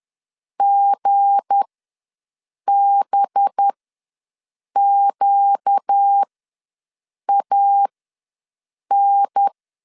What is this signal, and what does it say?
Morse code, heard as a keyed tone, about 10 wpm, spaced as for 6 wpm, 800 Hz. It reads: GBQAN
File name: LEZIONI / 7 GF9Z / 7 G ws 10wpm 3843